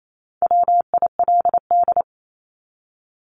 Morse code, heard as a keyed tone, about 28 wpm, 700 Hz. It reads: WILD
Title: Top 1501-1750 Words 28wpm